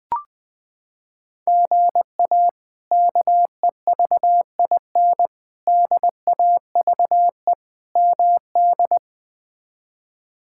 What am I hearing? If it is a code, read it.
Morse code, heard as a keyed tone, about 20 wpm, 700 Hz. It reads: GA KEVIN DAVE MD
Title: K1USN SST QSO Practice - 20wpm